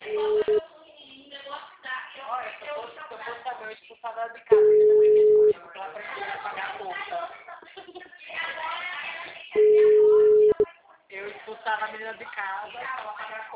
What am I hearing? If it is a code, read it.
Ringback tone (Europe, 425 Hz), with speech.